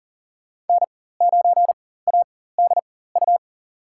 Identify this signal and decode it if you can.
Morse code, heard as a keyed tone, about 40 wpm, spaced as for 19 wpm, 700 Hz. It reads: N9ADU
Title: USA Call Signs - 5 Characters - 4x Character Spacing - 40wpm